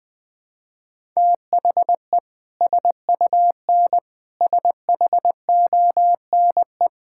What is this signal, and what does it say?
Morse code, heard as a keyed tone, about 20 wpm, 700 Hz. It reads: THE SUN SHONE
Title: Sentences with 2-Word Phrases - Spoken Beginning - 20wpm